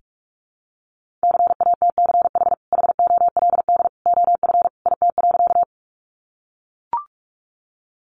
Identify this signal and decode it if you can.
Morse code, heard as a keyed tone, about 44 wpm, 700 Hz. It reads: CATCH HOLD OF IT.
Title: Sentences from Top 1500 Words 45wpm